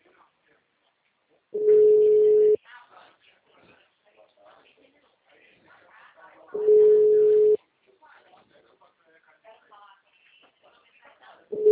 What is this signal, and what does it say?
Ringback tone (Europe, 425 Hz), with speech.